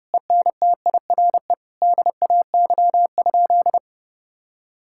Morse code, heard as a keyed tone, about 30 wpm, 700 Hz. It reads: ENTIRE DAY?
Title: Sentences from Top 1000 Words - Review Part 5 - 30wpm